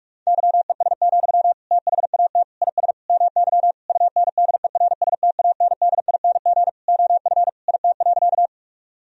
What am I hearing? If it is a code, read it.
Morse code, heard as a keyed tone, about 46 wpm, 700 Hz. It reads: YES, THAT IS MY UNDERSTANDING OF IT.
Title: Sentences from Top 1000 Words - Review Part 3 - 45wpm